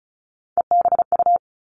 Morse code, heard as a keyed tone, about 36 wpm, 700 Hz. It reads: EBU
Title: Three Challenging Letter-Number Combinations 35wpm